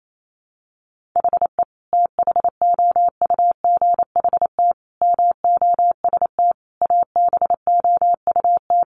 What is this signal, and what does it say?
Morse code, heard as a keyed tone, about 28 wpm, 700 Hz. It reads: HE THOUGHT MOST ABOUT
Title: Sentences from Top 1500 Words 28wpm